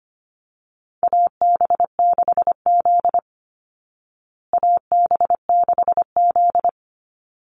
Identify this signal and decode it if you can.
Morse code, heard as a keyed tone, about 25 wpm, 700 Hz. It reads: AB6Z AB6Z
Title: Call Signs Repeated 3x - 25wpm